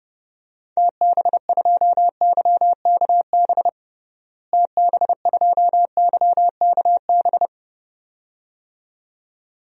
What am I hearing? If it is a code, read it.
Morse code, heard as a keyed tone, about 30 wpm, 700 Hz. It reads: TB2YKB TB2YKB
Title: DXCC-100 Call Signs - 6 Characters - Repeated 2x - 30wpm